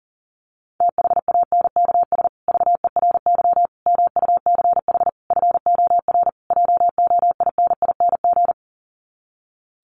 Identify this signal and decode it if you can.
Morse code, heard as a keyed tone, about 40 wpm, 700 Hz. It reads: THANKS VERY MUCH FOR JOINING
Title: Sets of 5 Words Encore2 40wpm